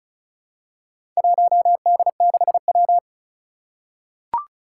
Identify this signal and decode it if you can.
Morse code, heard as a keyed tone, about 36 wpm, 700 Hz. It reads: 1D6W
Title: Four Challenging Letter-Number Combinations 35wpm